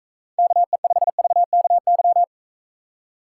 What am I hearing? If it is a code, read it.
Morse code, heard as a keyed tone, about 41 wpm, 700 Hz. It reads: KE5VKY